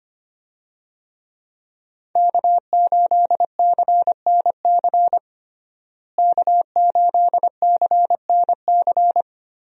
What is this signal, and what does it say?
Morse code, heard as a keyed tone, about 25 wpm, 700 Hz. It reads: K8CNC K8CNC
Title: USA Call Signs - 5 Characters - Repeated 3x - 25wpm